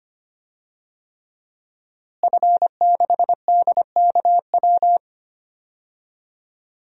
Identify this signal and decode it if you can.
Morse code, heard as a keyed tone, about 25 wpm, 700 Hz. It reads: F6DKW
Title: DXCC-100 Call Signs - 5 Characters - Repeated 3x - 25wpm